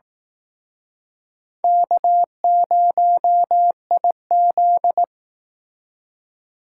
Morse code, heard as a keyed tone, about 18 wpm, 700 Hz. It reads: K0IZ
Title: Speed Racing - USA 1x2 Call Signs 15wpm